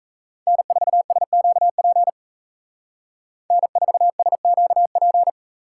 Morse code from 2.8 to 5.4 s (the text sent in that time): N4SQP